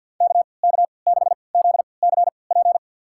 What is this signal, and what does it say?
Morse code, heard as a keyed tone, about 51 wpm, spaced as for 28 wpm, 700 Hz. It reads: KK6ZCP
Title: USA Call Signs - 3x Character Spacing - 50wpm